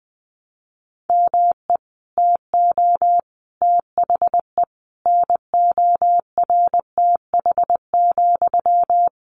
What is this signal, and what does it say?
Morse code, heard as a keyed tone, about 20 wpm, 700 Hz. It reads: ME TO THE NORTH,
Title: Sentences from Top 400 Words 20wpm